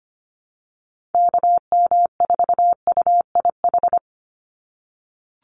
Morse code, heard as a keyed tone, about 25 wpm, 700 Hz. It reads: KM4UIH
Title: USA Call Signs 25wpm